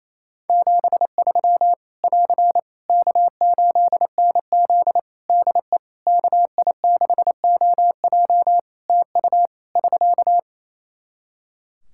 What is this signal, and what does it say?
Morse code, heard as a keyed tone, about 28 wpm, 700 Hz. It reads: 73 <AR> K8NZ DE KI6OJ TU <SK>
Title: Practice QSO - Exchange 4 - Ending - 28wpm